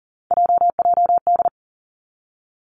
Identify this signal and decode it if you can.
Morse code, heard as a keyed tone, about 38 wpm, 700 Hz. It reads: JJD
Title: Three Challenging Letter-Number Combinations 40wpm